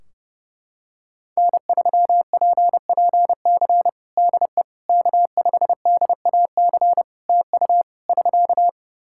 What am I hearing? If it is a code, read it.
Morse code, heard as a keyed tone, about 30 wpm, 700 Hz. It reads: N3PPC DE K5DAC TU <SK>
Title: K1USN SST QSO Practice 30wpm